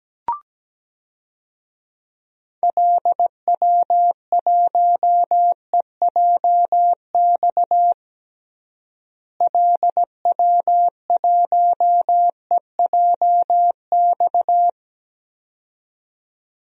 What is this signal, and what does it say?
Morse code, heard as a keyed tone, about 17 wpm, 700 Hz. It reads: LW1EJX LW1EJX
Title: DXCC-100 Call Signs - 6 Characters - Repeated 2x - 17wpm